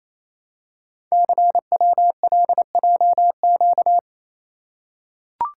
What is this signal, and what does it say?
Morse code, heard as a keyed tone, about 28 wpm, 700 Hz. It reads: CWLJQ
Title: Code Groups - 5 Letters - 28wpm